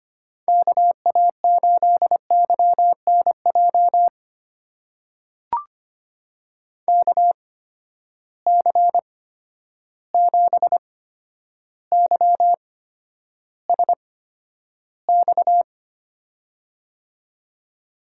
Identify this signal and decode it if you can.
Morse code, heard as a keyed tone, about 25 wpm, spaced as for 11 wpm, 700 Hz. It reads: KA8YNJ KC7YSX
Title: USA Call Signs - 6 Characters - 8x Character Spacing - 25wpm